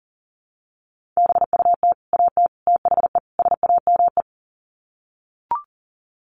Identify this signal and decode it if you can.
Morse code, heard as a keyed tone, about 40 wpm, 700 Hz. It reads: BUT AT THE SAME
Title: Sets of 4 Words 40wpm